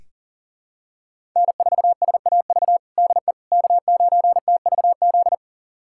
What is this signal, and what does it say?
Morse code, heard as a keyed tone, about 37 wpm, 700 Hz. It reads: N4SAV DE K9TVZ
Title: USA Callsign DE Callsign - 5 Characters - 40wpm